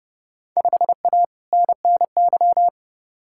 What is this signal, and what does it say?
Morse code, heard as a keyed tone, about 32 wpm, 700 Hz. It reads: 5A NNY
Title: Field Day Exchanges - 30wpm